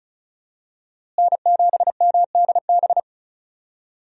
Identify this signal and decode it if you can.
Morse code, heard as a keyed tone, about 36 wpm, 700 Hz. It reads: N7MDB
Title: USA Call Signs - 5 Characters - Repeated 3x - 35wpm